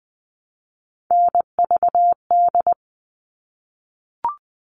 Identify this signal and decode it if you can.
Morse code, heard as a keyed tone, about 20 wpm, 700 Hz. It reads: NVD
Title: All Cut Numbers - 3-Digit 20wpm